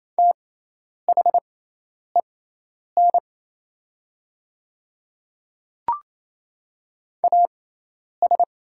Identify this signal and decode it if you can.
Morse code, heard as a keyed tone, about 27 wpm, spaced as for 10 wpm, 700 Hz. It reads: THEN AS
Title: 6x Character Spacing - Top 100 Words - 28wpm